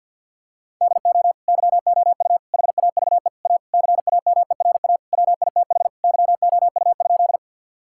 Morse code, heard as a keyed tone, about 50 wpm, 700 Hz. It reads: DO YOU HAVE A CAMERA WITH YOU?
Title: Sentences from Top 1000 Words - Review Part 5 - 50wpm